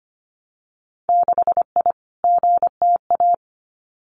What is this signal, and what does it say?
Morse code, heard as a keyed tone, about 25 wpm, 700 Hz. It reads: 6I GTA
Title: Winter Field Day Exchanges Encore 25wpm